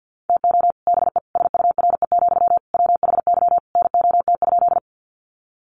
Morse code, heard as a keyed tone, about 49 wpm, 700 Hz. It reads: TO BE SURE, WHY NOT?
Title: Sentences from Top 600 Words 50wpm